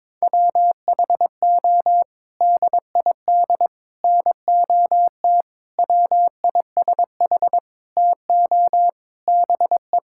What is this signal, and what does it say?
Morse code, heard as a keyed tone, about 22 wpm, 700 Hz. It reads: WHO DID NOT WISH TO BE
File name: practice / 22wpm / Sets of 7 Words - 22wpm